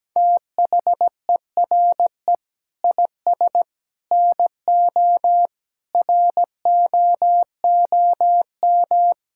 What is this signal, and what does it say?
Morse code, heard as a keyed tone, about 17 wpm, 700 Hz. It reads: THERE IS NO ROOM
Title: Sentences from Top 1000 Words - Review Part 5 - 17wpm